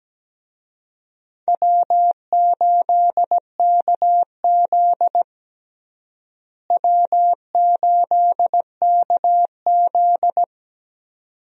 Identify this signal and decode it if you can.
Morse code, heard as a keyed tone, about 17 wpm, 700 Hz. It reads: W8KZ W8KZ
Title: USA Call Signs - 4 Characters - Repeated 2x - 17wpm